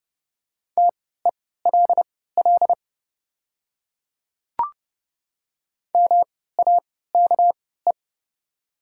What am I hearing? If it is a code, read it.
Morse code, heard as a keyed tone, about 30 wpm, spaced as for 17 wpm, 700 Hz. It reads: TELL MAKE
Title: 3x Character Spacing - Top 200 Words - 30wpm